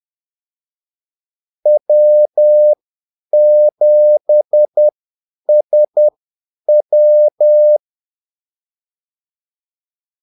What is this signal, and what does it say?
Morse code, heard as a keyed tone, about 10 wpm, spaced as for 8 wpm, 600 Hz. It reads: W7SW
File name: Callsigns-10-8wpm